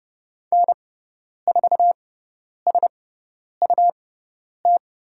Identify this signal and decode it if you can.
Morse code, heard as a keyed tone, about 30 wpm, spaced as for 10 wpm, 700 Hz. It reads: N4SUT
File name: Lesson 17g - Letter u - Call Signs 30wpm